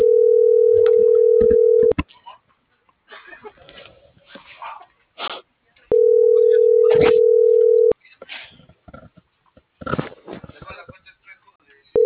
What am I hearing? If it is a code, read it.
Ringback tone (North America), 440 + 480 Hz.